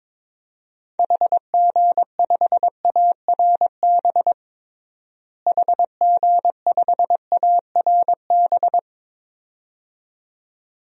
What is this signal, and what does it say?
Morse code, heard as a keyed tone, about 22 wpm, 700 Hz. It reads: HG5ARB HG5ARB
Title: DXCC-100 Call Signs - 6 Characters - Repeated 2x - 22wpm